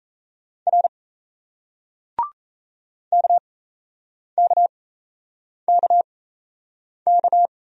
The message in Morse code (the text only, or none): R KKKK